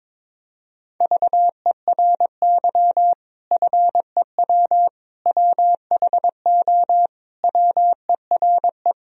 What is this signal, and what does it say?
Morse code, heard as a keyed tone, about 22 wpm, 700 Hz. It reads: VERY FEW WHO WERE
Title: Sentences from Top 1000 Words - Review Part 2 - 22wpm